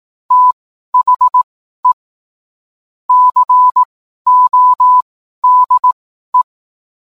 Morse code, heard as a keyed tone, about 18 wpm, spaced as for 12 wpm, 1000 Hz. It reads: THE CODE